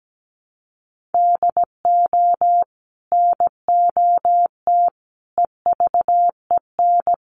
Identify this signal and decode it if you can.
Morse code, heard as a keyed tone, about 17 wpm, 700 Hz. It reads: DO NOT EVEN
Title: Sets of 5 Words Encore 17wpm